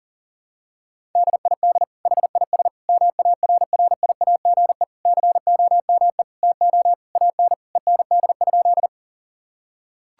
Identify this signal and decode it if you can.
Morse code, heard as a keyed tone, about 40 wpm, 700 Hz. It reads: DID HIS MARRIAGE COME TO AN END?
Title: Sentences from Top 1000 Words - Review Part 5 - 40wpm